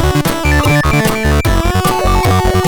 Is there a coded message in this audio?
No.